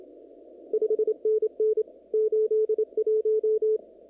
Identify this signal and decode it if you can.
Morse code, heard as a keyed tone, about 27 wpm, 430 Hz. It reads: 5NN 81
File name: CW / kt5j wpx cw-op-k5pi